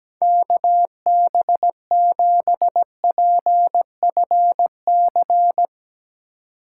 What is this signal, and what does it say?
Morse code, heard as a keyed tone, about 17 wpm, 700 Hz. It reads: KB7PFC
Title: USA Call Signs - 6 Characters - 2x Character Spacing - 17wpm